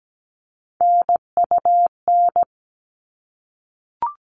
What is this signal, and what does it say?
Morse code, heard as a keyed tone, about 17 wpm, 700 Hz. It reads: NUN
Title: All Cut Numbers - 3-Digit 17wpm